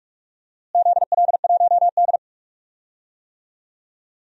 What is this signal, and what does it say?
Morse code, heard as a keyed tone, about 43 wpm, 700 Hz. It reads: ZL1D
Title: DXCC-100 Call Signs - 4 Characters - Repeated 3x - 45wpm